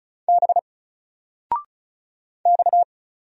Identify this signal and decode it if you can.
Morse code, heard as a keyed tone, about 35 wpm, 700 Hz. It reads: B X
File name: Single-Letter-Number-35wpm